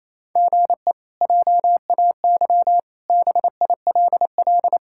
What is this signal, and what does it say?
Morse code, heard as a keyed tone, about 28 wpm, 700 Hz. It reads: GE JAY BILL
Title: K1USN SST QSO Practice - 28wpm